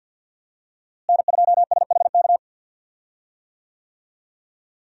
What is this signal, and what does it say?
Morse code, heard as a keyed tone, about 52 wpm, 700 Hz. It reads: N2SHK
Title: USA Call Signs - 5 Characters - Repeated 3x - 50wpm